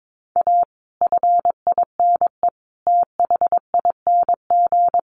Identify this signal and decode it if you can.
Morse code, heard as a keyed tone, about 22 wpm, 700 Hz. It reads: A FINE THING
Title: Sentences from Top 500 Words 22wpm